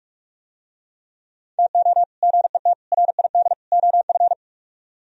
Morse code, heard as a keyed tone, about 47 wpm, 700 Hz. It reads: TO GET RID OF